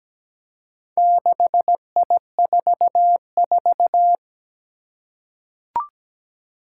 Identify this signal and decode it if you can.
Morse code, heard as a keyed tone, about 17 wpm, 700 Hz. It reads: 6I44
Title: Four Challenging Letter-Number Combinations 17wpm